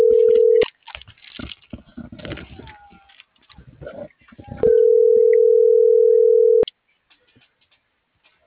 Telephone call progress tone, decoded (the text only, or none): ringback tone